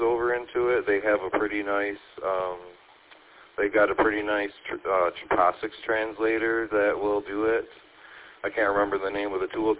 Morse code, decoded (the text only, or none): none